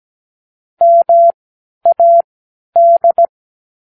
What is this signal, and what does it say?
Morse code, heard as a keyed tone, about 17 wpm, spaced as for 10 wpm, 680 Hz. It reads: MAD
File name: k7qo / Practicewords